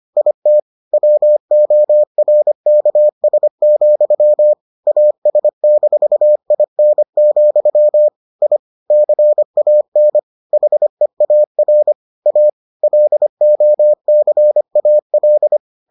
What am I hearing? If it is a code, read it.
Morse code, heard as a keyed tone, about 25 wpm, 600 Hz. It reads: IT WORKS, AS-IN, I CAN HEAR A LOCAL